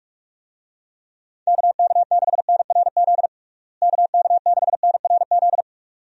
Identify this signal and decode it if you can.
Morse code, heard as a keyed tone, about 47 wpm, 700 Hz. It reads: KK6NRZ KK6NRZ
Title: USA Call Signs - 6 Characters - Repeated 2x - 45wpm